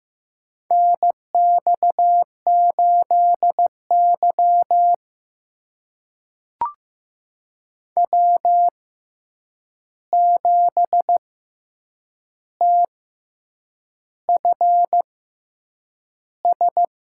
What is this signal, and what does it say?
Morse code, heard as a keyed tone, about 15 wpm, spaced as for 7 wpm, 700 Hz. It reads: NX8Y W7TFS
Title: USA Call Signs - 6x Character Spacing - 15wpm